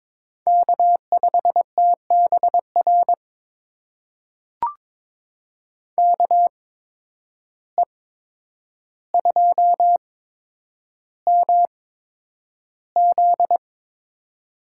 Morse code, heard as a keyed tone, about 22 wpm, spaced as for 9 wpm, 700 Hz. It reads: K5TBR KE2MZ
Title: USA Call Signs - 5 Characters - 8x Character Spacing - 22wpm